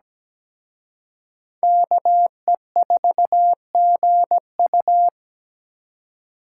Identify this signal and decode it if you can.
Morse code, heard as a keyed tone, about 20 wpm, 700 Hz. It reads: KE4GU
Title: Speed Racing - USA Call Signs 17wpm